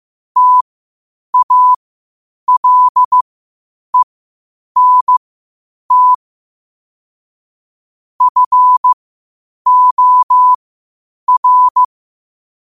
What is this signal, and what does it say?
Morse code, heard as a keyed tone, about 15 wpm, spaced as for 8 wpm, 1000 Hz. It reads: TALENT FOR
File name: 8wpm